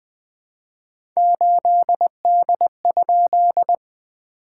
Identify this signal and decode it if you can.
Morse code, heard as a keyed tone, about 20 wpm, 700 Hz. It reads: 8D?
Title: SOTA QSO Practice 20wpm